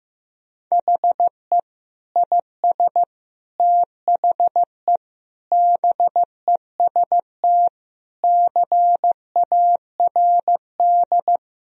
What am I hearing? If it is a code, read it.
Morse code, heard as a keyed tone, about 15 wpm, 700 Hz. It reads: HE IS THE BEST CARD
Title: Sentences from Top 1000 Words - Review Part 4 - 15wpm